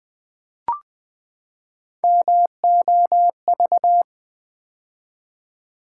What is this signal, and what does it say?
Morse code, heard as a keyed tone, about 20 wpm, 700 Hz. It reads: MOV
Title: Build A Word - 3 Letters At A Time - 8-Letter Words - 20wpm